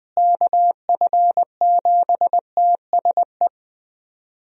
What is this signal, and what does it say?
Morse code, heard as a keyed tone, about 21 wpm, 700 Hz. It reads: KF7TSE
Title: USA Call Signs - 6 Characters - Repeated 3x - 20wpm